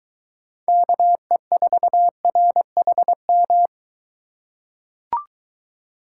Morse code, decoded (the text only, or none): KE4RHM